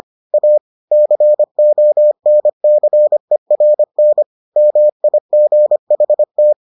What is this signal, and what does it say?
Morse code, heard as a keyed tone, about 25 wpm, 600 Hz. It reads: A CONCERN MIGHT